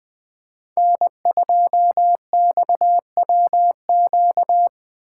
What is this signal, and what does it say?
Morse code, heard as a keyed tone, about 20 wpm, 700 Hz. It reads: N2XWQ
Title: Speed Racing - USA 1x3 Call Signs 17wpm